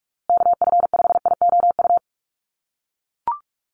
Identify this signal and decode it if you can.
Morse code, heard as a keyed tone, about 44 wpm, 700 Hz. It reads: KF5IOU
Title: USA Call Signs Repeated 2x 45wpm